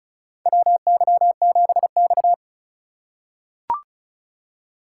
Morse code, heard as a keyed tone, about 36 wpm, 700 Hz. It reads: WY7X
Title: USA Call Signs - 4 Characters - 3x Character Spacing - 35wpm